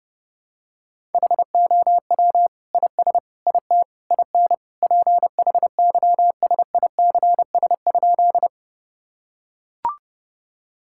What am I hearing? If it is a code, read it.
Morse code, heard as a keyed tone, about 30 wpm, 700 Hz. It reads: HOW IS IT IN PHYSICS?
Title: Sentences from Top 1000 Words - Review Part 5 - 30wpm